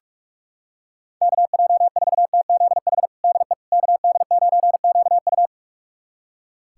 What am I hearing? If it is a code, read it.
Morse code, heard as a keyed tone, about 41 wpm, 700 Hz. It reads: KJ4TZH DE KD9QU
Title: USA Callsign DE Callsign 45wpm